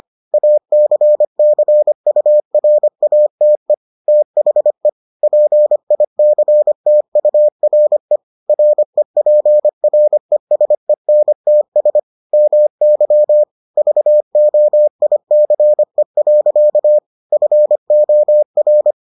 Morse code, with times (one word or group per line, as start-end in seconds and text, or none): 0.3-3.7 ACCURATE
4.1-4.9 THE
5.2-8.2 PICTURE
8.5-12.0 REPRESENTS
12.3-13.4 MY
13.8-17.0 VOICE.
17.3-18.9 FOR